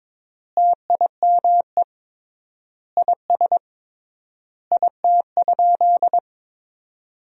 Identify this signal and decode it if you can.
Morse code, heard as a keyed tone, about 22 wpm, 700 Hz. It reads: TIME IS IT?
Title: 3x Word-Spacing - Sentences from Top 100 Words - 22wpm